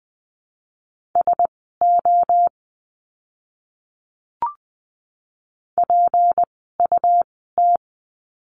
Morse code, heard as a keyed tone, about 20 wpm, spaced as for 15 wpm, 700 Hz. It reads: SO PUT
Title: 230 Lesson 37f - Letter x - Words 20wpm